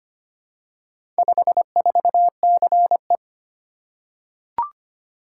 Morse code, heard as a keyed tone, about 25 wpm, 700 Hz. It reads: 54CE
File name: practice / 25wpm / Four Letter-Number Combinations - 25wpm